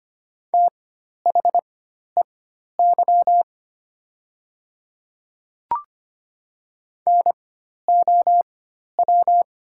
Morse code, heard as a keyed tone, about 25 wpm, spaced as for 12 wpm, 700 Hz. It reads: THEY NOW
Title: 4x Character Spacing - Top 100 Words - 25wpm